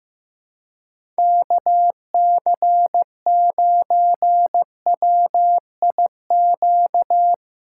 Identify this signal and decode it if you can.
Morse code, heard as a keyed tone, about 15 wpm, 700 Hz. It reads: KC9WIQ
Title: USA Callsign DE Callsign - 6 Characters - 15wpm